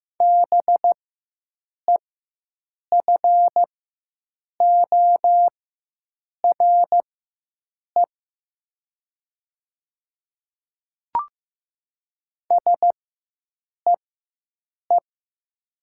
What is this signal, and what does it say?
Morse code, heard as a keyed tone, about 15 wpm, spaced as for 7 wpm, 700 Hz. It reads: BEFORE SEE
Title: 4x Character Spacing - Top 200 Words - 15wpm